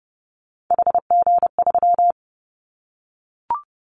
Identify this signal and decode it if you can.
Morse code, heard as a keyed tone, about 30 wpm, 700 Hz. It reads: HG3
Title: Callsign Prefixes for Top 50 DXCC - 30wpm